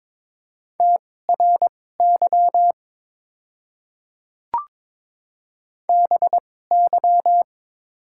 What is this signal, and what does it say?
Morse code, heard as a keyed tone, about 22 wpm, spaced as for 16 wpm, 700 Hz. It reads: TRY BY
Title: 2x Character Spacing - Top 200 Words - 22wpm